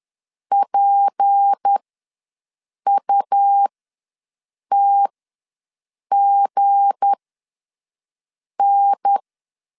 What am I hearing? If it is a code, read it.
Morse code, heard as a keyed tone, about 10 wpm, spaced as for 5 wpm, 800 Hz. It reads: PUTGN